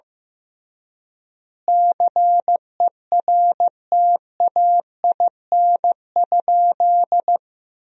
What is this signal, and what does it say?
Morse code, heard as a keyed tone, about 15 wpm, 700 Hz. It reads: CERTAIN?
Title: 3x Word-Spacing - Sentences from Top 500 Words - 15wpm